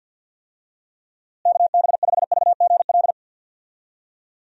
Morse code, heard as a keyed tone, about 51 wpm, 700 Hz. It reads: KB5VGL
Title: USA Call Signs - 6 Characters - Repeated 3x - 50wpm